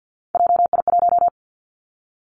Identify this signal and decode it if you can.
Morse code, heard as a keyed tone, about 50 wpm, 700 Hz. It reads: JI1
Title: Three Challenging Letter-Number Combinations 50wpm